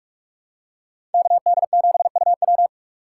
Morse code, heard as a keyed tone, about 46 wpm, 700 Hz. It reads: KD7UW
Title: USA Call Signs - 5 Characters - 4x Character Spacing - 45wpm